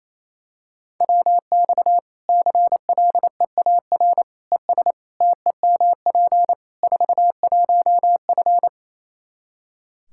Morse code, heard as a keyed tone, about 28 wpm, 700 Hz. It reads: WX CLEAR ES TEMP 41F
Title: K1USN SST QSO Practice 28wpm